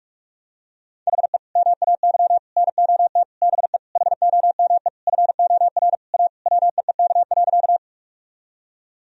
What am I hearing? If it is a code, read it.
Morse code, heard as a keyed tone, about 42 wpm, 700 Hz. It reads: HE MAY NOT BE HOME FOR A WEEK.
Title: Sentences from Top 1000 Words - Review Part 2 - 45wpm